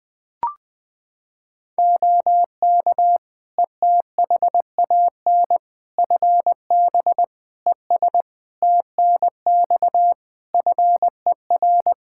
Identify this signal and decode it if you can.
Morse code, heard as a keyed tone, about 20 wpm, 700 Hz. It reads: OK ETHAN FB ES TNX FER
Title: Practice QSO - Exchange 2 - Info - 20wpm